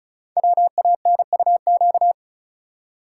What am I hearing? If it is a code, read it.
Morse code, heard as a keyed tone, about 36 wpm, 700 Hz. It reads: WANUQ
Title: Code Groups - 5 Letter:Numbers - 35wpm